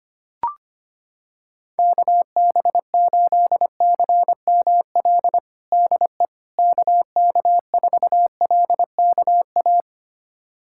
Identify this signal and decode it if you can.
Morse code, heard as a keyed tone, about 25 wpm, 700 Hz. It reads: KB8CML DE KK4LKA
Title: USA Callsign DE Callsign 25wpm